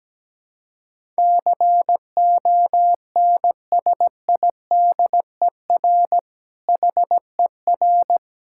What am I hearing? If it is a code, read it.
Morse code, heard as a keyed tone, about 17 wpm, 700 Hz. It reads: CONSIDER HER
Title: Sentences from Top 1000 Words - Review Part 6 - 17wpm